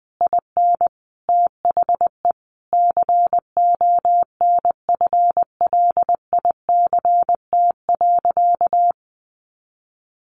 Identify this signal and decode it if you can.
Morse code, heard as a keyed tone, about 20 wpm, 700 Hz. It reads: IN THE CONFLICT.
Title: Sentences from Top 1300 Words 20wpm